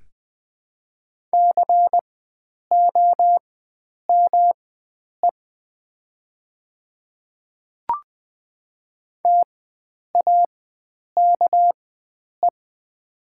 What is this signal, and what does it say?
Morse code, heard as a keyed tone, about 20 wpm, spaced as for 9 wpm, 700 Hz. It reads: COME TAKE
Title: 4x Character Spacing - Top 300 Words - 20wpm